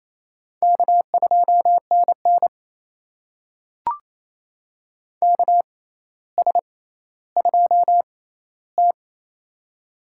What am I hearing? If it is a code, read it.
Morse code, heard as a keyed tone, about 28 wpm, spaced as for 14 wpm, 700 Hz. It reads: K2NN KS2T
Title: USA Call Signs - 4 Characters - 6x Character Spacing - 28wpm